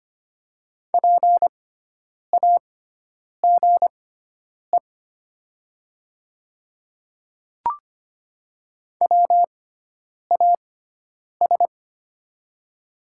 Morse code, heard as a keyed tone, about 25 wpm, spaced as for 9 wpm, 700 Hz. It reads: PAGE WAS
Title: 6x Character Spacing - Top 300 Words - 25wpm